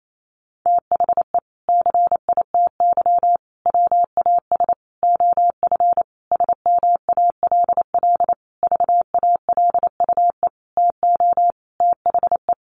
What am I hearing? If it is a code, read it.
Morse code, heard as a keyed tone, about 28 wpm, 700 Hz. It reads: THE CITY WAS OF SMALL VALUE TO THE
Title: Sentences from Top 1500 Words 28wpm